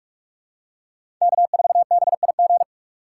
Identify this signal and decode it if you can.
Morse code, heard as a keyed tone, about 46 wpm, 700 Hz. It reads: K4BIG
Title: USA Call Signs - 5 Characters - 2x Character Spacing - 45wpm